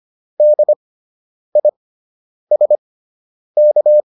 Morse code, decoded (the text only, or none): DISK